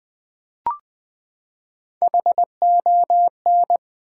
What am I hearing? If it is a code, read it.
Morse code, heard as a keyed tone, about 20 wpm, 700 Hz. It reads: HON
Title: Build A Word - 3 Letters At A Time - 6-Letter Words - 20wpm